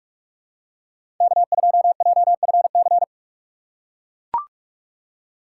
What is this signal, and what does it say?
Morse code, heard as a keyed tone, about 42 wpm, 700 Hz. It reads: K2JFC
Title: USA Call Signs - 4x Character Spacing - 45wpm